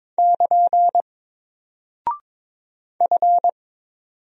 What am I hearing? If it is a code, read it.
Morse code, heard as a keyed tone, about 22 wpm, 700 Hz. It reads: <KN> F
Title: Single Character (Letter:Number:Punctuation:Prosign) Alt 22wpm